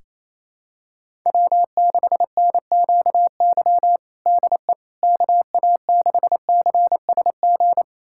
Morse code, heard as a keyed tone, about 28 wpm, 700 Hz. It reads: W6NQY DE KA6CSG
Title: USA Callsign DE Callsign 28wpm